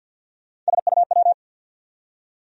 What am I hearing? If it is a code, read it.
Morse code, heard as a keyed tone, about 53 wpm, 700 Hz. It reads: SUW
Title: USA Call Sign Postfixes - 3 Letters 50wpm